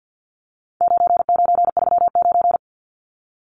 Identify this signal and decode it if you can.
Morse code, heard as a keyed tone, about 50 wpm, 700 Hz. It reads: 9939
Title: Numbers - 4-Digit 50wpm